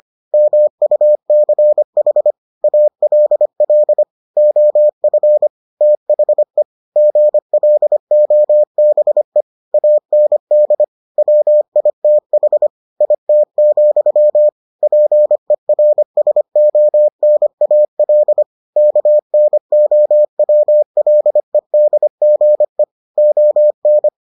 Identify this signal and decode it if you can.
Morse code, heard as a keyed tone, about 25 wpm, 600 Hz. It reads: MUCH ALL OF THE GLOBE AND WITH IT, PERSONAL KNOWLEDGE ON